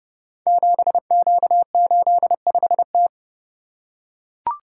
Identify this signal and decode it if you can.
Morse code, heard as a keyed tone, about 28 wpm, 700 Hz. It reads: 7Q85T
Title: Code Groups - 5 Letter:Numbers - 30wpm